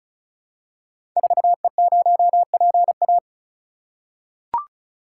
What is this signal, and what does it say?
Morse code, heard as a keyed tone, about 36 wpm, 700 Hz. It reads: 4E0PA
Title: Code Groups - 5 Letter:Numbers - 35wpm